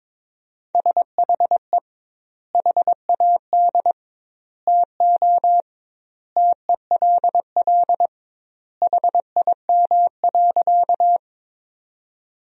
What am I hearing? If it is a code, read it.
Morse code, heard as a keyed tone, about 22 wpm, 700 Hz. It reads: SHE HAD TO TELL HIM.
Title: 2x Word-Spacing - Sentences from Top 200 Words - 22wpm